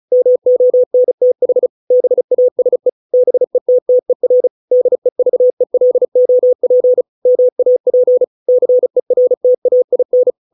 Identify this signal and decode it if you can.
Morse code, heard as a keyed tone, about 34 wpm, 500 Hz. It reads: MONTH BASE BETTER DEVELOP MAP CERTAIN